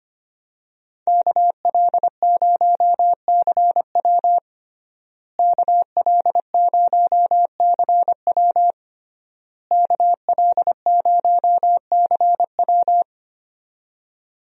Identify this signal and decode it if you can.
Morse code, heard as a keyed tone, about 25 wpm, 700 Hz. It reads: KL0CW KL0CW KL0CW
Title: USA Call Signs - 5 Characters - Repeated 3x - 25wpm